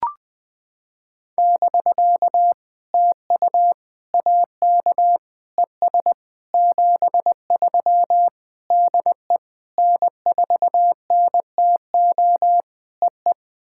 Morse code, heard as a keyed tone, about 20 wpm, 700 Hz. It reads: <BK> TU AK ES 73 DE N4NTO EE
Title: POTA QSO Practice - 20wpm